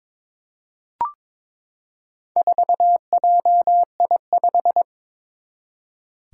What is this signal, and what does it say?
Morse code, heard as a keyed tone, about 22 wpm, 700 Hz. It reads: 4JI5
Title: Four Challenging Letter-Number Combinations 22wpm